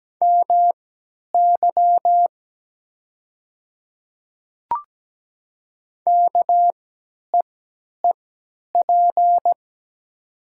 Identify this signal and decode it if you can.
Morse code, heard as a keyed tone, about 17 wpm, spaced as for 10 wpm, 690 Hz. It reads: MY KEEP
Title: 3x Character Spacing - Top 500 Words - 17wpm